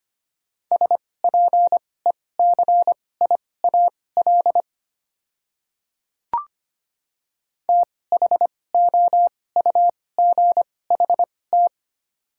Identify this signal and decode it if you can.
Morse code, heard as a keyed tone, about 25 wpm, spaced as for 18 wpm, 700 Hz. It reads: SPECIAL THOUGHT
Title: 2x Character Spacing - Top 500 Words - 25wpm